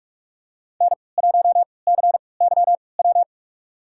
Morse code, heard as a keyed tone, about 43 wpm, spaced as for 25 wpm, 700 Hz. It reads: N1CYW
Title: USA Call Signs - 5 Characters - 3x Character Spacing - 45wpm